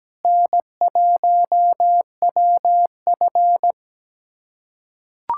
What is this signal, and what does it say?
Morse code, heard as a keyed tone, about 17 wpm, 700 Hz. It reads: N1WF
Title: USA Callsign DE Callsign - 4 Characters - 17wpm